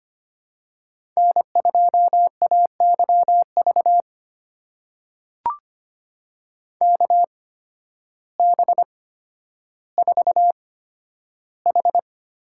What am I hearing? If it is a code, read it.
Morse code, heard as a keyed tone, about 25 wpm, spaced as for 12 wpm, 700 Hz. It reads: N2AYV KB4H
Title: USA Call Signs - 8x Character Spacing - 25wpm